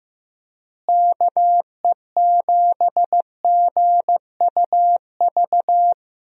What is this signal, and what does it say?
Morse code, heard as a keyed tone, about 15 wpm, 700 Hz. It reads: KE7GUV
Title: USA Call Signs - 6 Characters - Repeated 2x - 15wpm